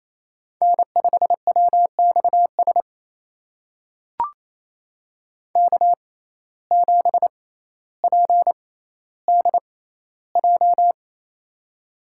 Morse code, heard as a keyed tone, about 28 wpm, spaced as for 14 wpm, 700 Hz. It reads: N5WXS K7PDJ
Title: USA Call Signs - 5 Characters - 6x Character Spacing - 28wpm